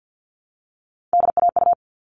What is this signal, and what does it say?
Morse code, heard as a keyed tone, about 50 wpm, 700 Hz. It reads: DAU
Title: All Cut Numbers - 3-Digit 50wpm